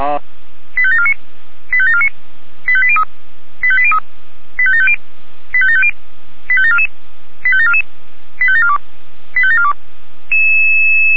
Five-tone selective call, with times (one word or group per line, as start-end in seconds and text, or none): 0.8-1.1 85749
1.7-2.1 85749
2.7-3.0 85803
3.6-4.0 85803
4.6-5.0 85750
5.5-5.9 85750
6.5-6.9 85744
7.4-7.8 85744
8.4-8.8 85742
9.4-9.7 85742